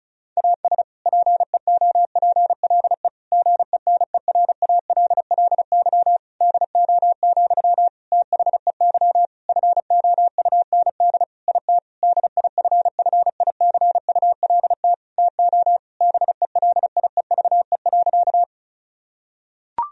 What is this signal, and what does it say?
Morse code, heard as a keyed tone, about 34 wpm, 700 Hz. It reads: AS PEOPLE GENERALLY DO, THEY FOUND IT DIFFICULT TO BELIEVE.
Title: Sentences from Top 1000 Words - Review Part 4 - 35wpm